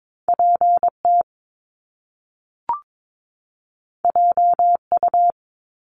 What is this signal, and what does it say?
Morse code, heard as a keyed tone, about 22 wpm, 700 Hz. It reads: PT JU
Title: Top Two Letter Combinations in Words 22wpm